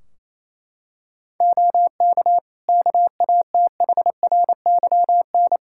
Morse code, heard as a keyed tone, about 28 wpm, 700 Hz. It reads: OK KATHRYN
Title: K1USN SST QSO Practice 28wpm